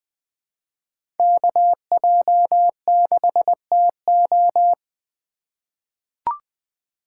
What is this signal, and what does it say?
Morse code, heard as a keyed tone, about 20 wpm, 700 Hz. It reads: KJ6TO
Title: USA Call Signs - 2x Character Spacing - 20wpm